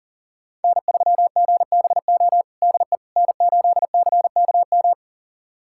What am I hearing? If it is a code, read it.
Morse code, heard as a keyed tone, about 40 wpm, 700 Hz. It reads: N3GBO DE N8CKM
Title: USA Callsign DE Callsign - 5 Characters - 40wpm